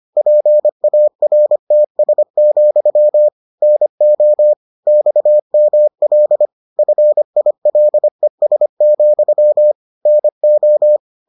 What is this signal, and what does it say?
Morse code, heard as a keyed tone, about 25 wpm, 600 Hz. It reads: PARTS, NO XML FILES, NO